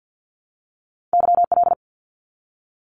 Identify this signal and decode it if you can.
Morse code, heard as a keyed tone, about 50 wpm, 700 Hz. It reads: YL
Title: Abbreviations 50wpm